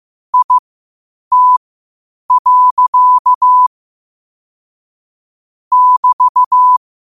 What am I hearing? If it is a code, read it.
Morse code, heard as a keyed tone, about 15 wpm, spaced as for 8 wpm, 1000 Hz. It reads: IT. <BT>
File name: qotdfiles / 8wpm